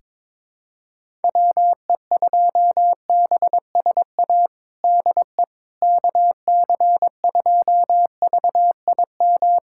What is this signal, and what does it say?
Morse code, heard as a keyed tone, about 22 wpm, 700 Hz. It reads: WE2BSA DE KC2VIM
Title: USA Callsign DE Callsign - 6 Characters - 22wpm